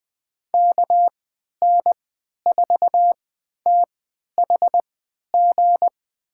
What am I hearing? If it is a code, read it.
Morse code, heard as a keyed tone, about 20 wpm, spaced as for 11 wpm, 700 Hz. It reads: KN4THG
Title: USA Call Signs - 6 Characters - 3x Character Spacing - 20wpm